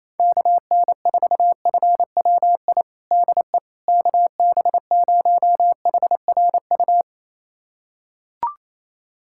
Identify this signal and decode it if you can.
Morse code, heard as a keyed tone, about 28 wpm, 700 Hz. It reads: KN4FWI DE KB0HRU
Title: USA Callsign DE Callsign - 6 Characters - 28wpm